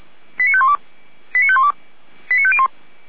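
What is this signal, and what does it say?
Five-tone selective call (ZVEI), heard as quick runs of five tones, tones of about 70 ms. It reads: 89512 89512 89501
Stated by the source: ZVEI five-tone call 89001 received by SDR, from the fire brigade archive of Feuerwehr Orlamünde.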